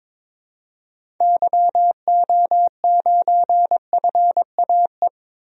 Morse code, heard as a keyed tone, about 21 wpm, 700 Hz. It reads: YO9FAE
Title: DXCC-100 Call Signs - 6 Characters - 22wpm